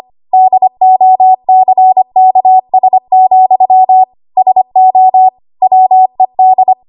Morse code, heard as a keyed tone, about 25 wpm, 750 Hz. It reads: DOCKS, SO WED
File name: Archive / 25 WPM / 25WPM